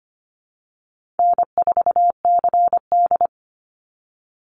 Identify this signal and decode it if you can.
Morse code, heard as a keyed tone, about 25 wpm, 700 Hz. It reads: N4CD
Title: Easy Call Signs - 25wpm